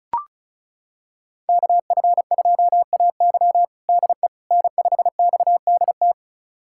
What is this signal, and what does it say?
Morse code, heard as a keyed tone, about 36 wpm, 700 Hz. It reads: KF2AY DE N5XDT
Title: USA Callsign DE Callsign - 5 Characters - 35wpm